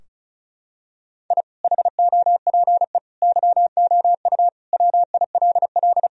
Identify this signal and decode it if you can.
Morse code, heard as a keyed tone, about 36 wpm, 700 Hz. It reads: I HOPE YOU WILL
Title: Sentences from Top 1000 Words - Review Part 4 - 35wpm